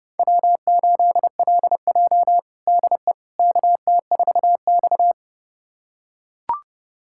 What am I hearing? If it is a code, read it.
Morse code, heard as a keyed tone, about 30 wpm, 700 Hz. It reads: W8LJ DE KT4X
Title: USA Callsign DE Callsign - 4 Characters - 30wpm